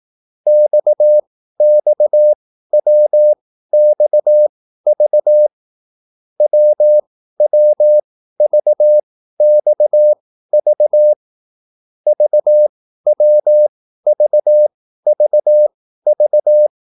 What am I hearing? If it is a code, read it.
Morse code, heard as a keyed tone, about 18 wpm, spaced as for 13 wpm, 600 Hz. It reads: XXWXV WWVXV VWVVV